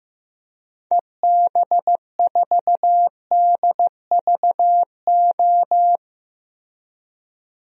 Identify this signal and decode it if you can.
Morse code, heard as a keyed tone, about 16 wpm, 700 Hz. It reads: EB4DVO
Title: DXCC-100 Call Signs - 6 Characters - 15wpm